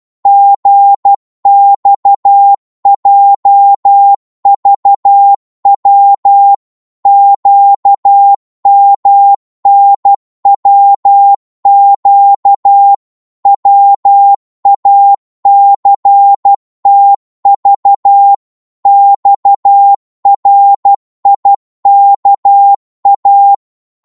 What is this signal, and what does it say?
Morse code, heard as a keyed tone, about 12 wpm, 800 Hz. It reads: GXJVW QMNWQ WACTV XRIKA
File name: Morse / ltr-12wpm